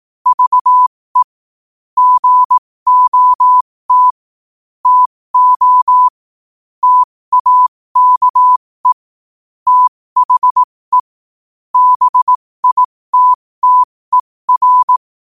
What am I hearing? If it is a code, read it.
Morse code, heard as a keyed tone, about 18 wpm, spaced as for 15 wpm, 1000 Hz. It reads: VE GOT TO TAKE THE BITTER
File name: qotdfiles / 15wpm